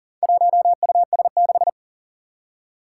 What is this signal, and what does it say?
Morse code, heard as a keyed tone, about 40 wpm, 700 Hz. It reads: 1US6
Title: Four Challenging Letter-Number Combinations 40wpm